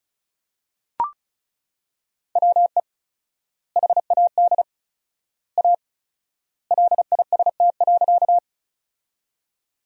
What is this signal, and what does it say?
Morse code, heard as a keyed tone, about 36 wpm, 700 Hz. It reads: WE HAD A LIST.
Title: 4x Word-Spacing - Sentences from Top 400 Words - 35wpm